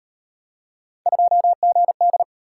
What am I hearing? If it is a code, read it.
Morse code, heard as a keyed tone, about 38 wpm, 700 Hz. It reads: 2GD